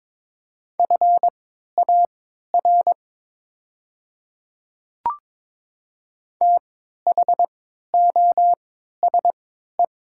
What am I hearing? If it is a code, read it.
Morse code, heard as a keyed tone, about 22 wpm, spaced as for 13 wpm, 700 Hz. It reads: FAR THOSE
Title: 3x Character Spacing - Top 300 Words - 22wpm